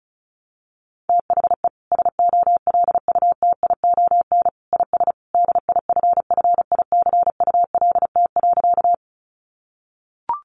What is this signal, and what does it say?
Morse code, heard as a keyed tone, about 34 wpm, 700 Hz. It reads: THE SOLUTION IS DIFFICULT.
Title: Sentences from Top 1300 Words 35wpm